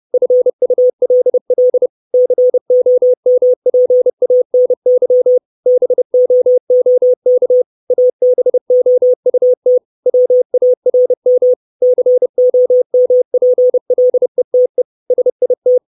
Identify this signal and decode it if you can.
Morse code, heard as a keyed tone, about 30 wpm, 500 Hz. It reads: FULL COMPANY BOOK ABOUT WARM COMPLETE SIT
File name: 632 Words 30WPM